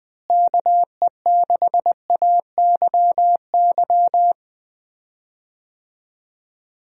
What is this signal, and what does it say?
Morse code, heard as a keyed tone, about 20 wpm, 700 Hz. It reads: KE6AYY
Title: USA Call Signs - 6 Characters - Repeated 2x - 20wpm